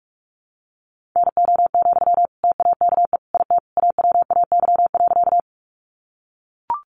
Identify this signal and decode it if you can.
Morse code, heard as a keyed tone, about 44 wpm, 700 Hz. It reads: NO, TAKE IT AWAY.
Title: Sentences from Top 600 Words 45wpm